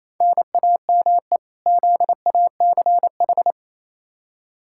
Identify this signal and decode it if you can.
Morse code, heard as a keyed tone, about 28 wpm, 700 Hz. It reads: NAME ZACH
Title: K1USN SST QSO Practice 28wpm